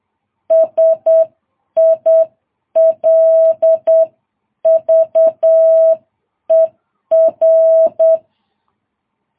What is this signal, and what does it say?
Morse code, heard as a keyed tone, about 8 wpm, 660 Hz. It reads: SILVER
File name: jh8cbh